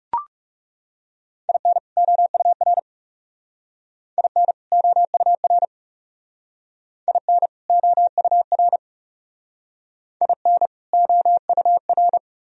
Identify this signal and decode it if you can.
Morse code, heard as a keyed tone, about 40 wpm, 700 Hz. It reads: IN OUR IN OUR IN OUR IN OUR